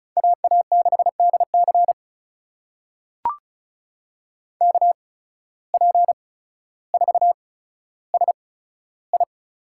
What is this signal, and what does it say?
Morse code, heard as a keyed tone, about 36 wpm, spaced as for 15 wpm, 700 Hz. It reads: AA6DC KP4SI
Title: USA Call Signs - 8x Character Spacing - 35wpm